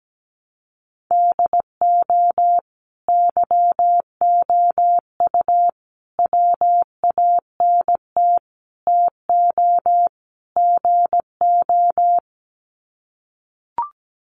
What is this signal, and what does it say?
Morse code, heard as a keyed tone, about 17 wpm, 700 Hz. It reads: DO YOU WANT TO GO
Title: Sets of 5 Words Encore2 17wpm